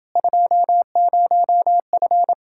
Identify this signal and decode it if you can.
Morse code, heard as a keyed tone, about 30 wpm, 700 Hz. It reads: 20F